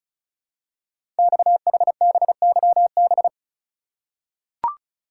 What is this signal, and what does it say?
Morse code, heard as a keyed tone, about 33 wpm, 700 Hz. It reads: XHBYB
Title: Code Groups - 5 Letters - 35wpm